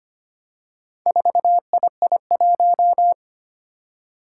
Morse code, heard as a keyed tone, about 25 wpm, 700 Hz. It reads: 4II1
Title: Four Challenging Letter-Number Combinations 25wpm